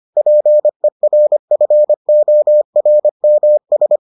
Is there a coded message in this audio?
Yes: Morse code from 0.2 to 4.0 s, heard as a keyed tone, about 25 wpm, 600 Hz. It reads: PERFORMS